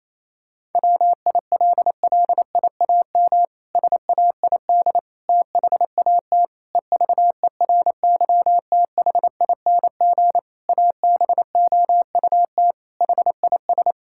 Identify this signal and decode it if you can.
Morse code, heard as a keyed tone, about 28 wpm, 700 Hz. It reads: WILLIAM SAID THAT EVERYTHING ABOUT HIS
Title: Pangram Challenge v2 - 28wpm